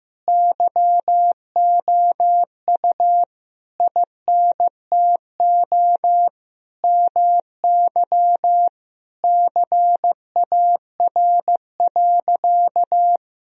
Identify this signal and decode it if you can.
Morse code, heard as a keyed tone, about 15 wpm, 700 Hz. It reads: YOU INTO MY CAR.
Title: Sentences from Top 1000 Words - Review Part 1 - 15wpm